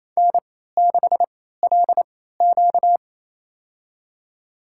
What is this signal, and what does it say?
Morse code, heard as a keyed tone, about 27 wpm, spaced as for 16 wpm, 700 Hz. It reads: N6LQ
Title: USA Call Signs - 4 Characters - 3x Character Spacing - 28wpm